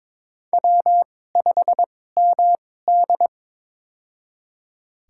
Morse code, heard as a keyed tone, about 23 wpm, spaced as for 16 wpm, 700 Hz. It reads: W5MD
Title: USA Call Signs - 4 Characters - 2x Character Spacing - 22wpm